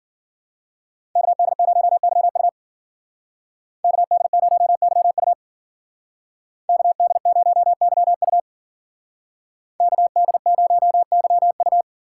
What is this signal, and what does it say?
Morse code, heard as a keyed tone, about 52 wpm, 700 Hz. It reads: KD0YU KD0YU KD0YU KD0YU